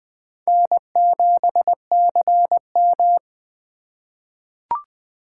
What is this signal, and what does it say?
Morse code, heard as a keyed tone, about 20 wpm, 700 Hz. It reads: N7CM
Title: USA Call Signs - 4 Characters - 3x Character Spacing - 20wpm